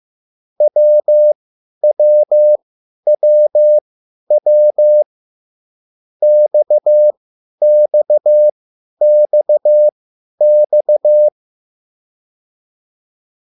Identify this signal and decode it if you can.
Morse code, heard as a keyed tone, about 15 wpm, spaced as for 10 wpm, 600 Hz. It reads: WWWW XXXX